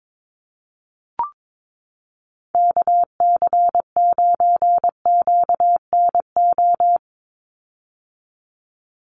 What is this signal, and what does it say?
Morse code, heard as a keyed tone, about 22 wpm, 700 Hz. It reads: KC9QNO
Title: USA Call Signs 22wpm